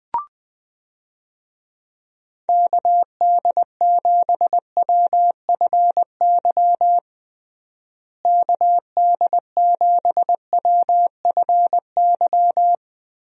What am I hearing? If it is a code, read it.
Morse code, heard as a keyed tone, about 20 wpm, 700 Hz. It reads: KD7WFY KD7WFY
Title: USA Call Signs - 6 Characters - Repeated 2x - 20wpm